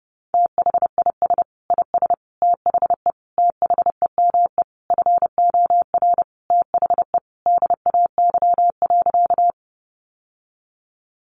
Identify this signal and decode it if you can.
Morse code, heard as a keyed tone, about 30 wpm, 700 Hz. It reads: THIS IS THE THEME FOR THE DAY.
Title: Sentences from Top 1100 Words 30wpm